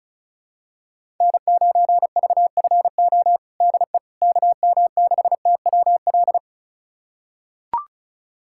Morse code, heard as a keyed tone, about 34 wpm, 700 Hz. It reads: N9VFO DE KM6TWL
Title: USA Callsign DE Callsign 35wpm